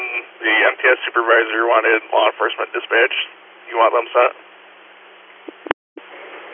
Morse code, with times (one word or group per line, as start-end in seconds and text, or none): none